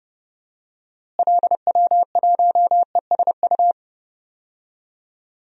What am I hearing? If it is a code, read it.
Morse code, heard as a keyed tone, about 30 wpm, 700 Hz. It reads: LW1ESU
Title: DXCC-100 Call Signs - 6 Characters - Repeated 3x - 30wpm